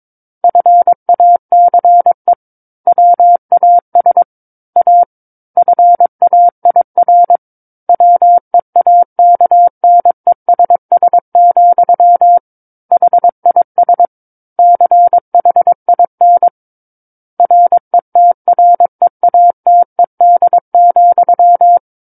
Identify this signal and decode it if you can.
Morse code, heard as a keyed tone, about 22 wpm, 700 Hz. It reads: FACE WAS A FAIR WEAKNESS, HIS CHIN RETREATED,
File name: Book One / 13-Chapter 13 - 22 WPM